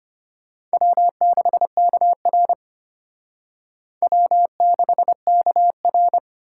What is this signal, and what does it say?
Morse code, heard as a keyed tone, about 32 wpm, 700 Hz. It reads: W6KR W6KR